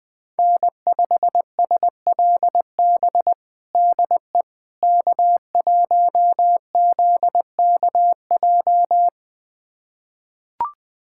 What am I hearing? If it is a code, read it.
Morse code, heard as a keyed tone, about 20 wpm, 700 Hz. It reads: N5SLB DE K1ZKJ
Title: USA Callsign DE Callsign 20wpm